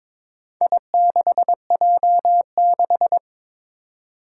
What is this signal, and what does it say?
Morse code, heard as a keyed tone, about 22 wpm, 700 Hz. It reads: I6J6
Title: Four Challenging Letter-Number Combinations 22wpm